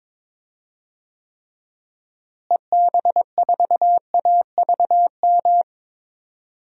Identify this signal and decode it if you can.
Morse code, heard as a keyed tone, about 22 wpm, 700 Hz. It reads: EB4AVM
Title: DXCC-100 Call Signs - 6 Characters - Repeated 2x - 22wpm